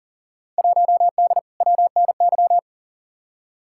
Morse code, heard as a keyed tone, about 40 wpm, 700 Hz. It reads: 1D WNY
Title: Field Day Exchanges - 40wpm